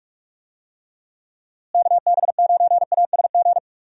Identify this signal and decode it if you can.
Morse code, heard as a keyed tone, about 46 wpm, 700 Hz. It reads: KB9ASG
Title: USA Call Signs - 6 Characters - Repeated 3x - 45wpm